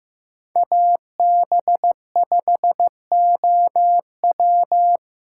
Morse code, heard as a keyed tone, about 18 wpm, 700 Hz. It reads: AB5OW
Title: Speed Racing - USA 2x2 Call Signs 15wpm